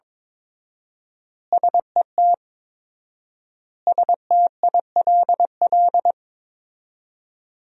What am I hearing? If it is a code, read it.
Morse code, heard as a keyed tone, about 22 wpm, 700 Hz. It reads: SET STILL
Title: 4x Word-Spacing - Sentences from Top 300 Words - 22wpm